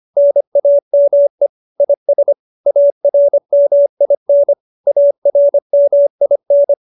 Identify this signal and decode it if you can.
Morse code, heard as a keyed tone, about 25 wpm, 570 Hz. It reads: NAME IS ARMIN ARMIN